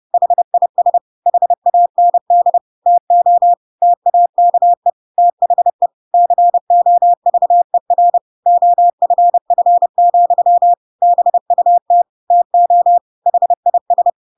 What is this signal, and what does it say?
Morse code, heard as a keyed tone, about 30 wpm, 700 Hz. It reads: HIS HAND TO TAKE THE COVER OFF, BUT TO HIS